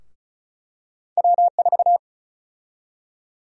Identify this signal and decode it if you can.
Morse code, heard as a keyed tone, about 36 wpm, 700 Hz. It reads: W4
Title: USA Call Sign Prefixes - 35wpm